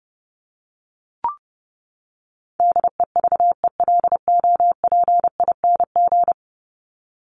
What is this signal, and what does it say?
Morse code, heard as a keyed tone, about 30 wpm, 700 Hz. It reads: DEVELOPING
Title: ING Suffix - Top 100 Words with Base Words 30wpm